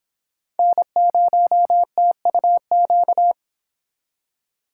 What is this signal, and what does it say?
Morse code, heard as a keyed tone, about 31 wpm, 700 Hz. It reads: N0TUQ